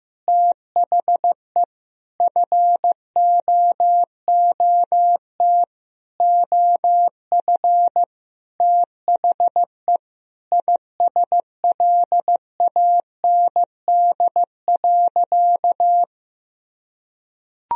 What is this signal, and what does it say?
Morse code, heard as a keyed tone, about 15 wpm, 700 Hz. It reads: THE FOOT OF THE ISLAND.
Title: Sentences from Top 1000 Words - Review Part 3 - 15wpm